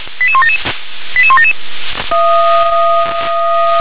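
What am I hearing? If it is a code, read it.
Five-tone selective call (ZVEI), heard as quick runs of five tones, tones of about 70 ms. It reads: 88177 88177
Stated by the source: Five-tone ZVEI selective call received by SDR.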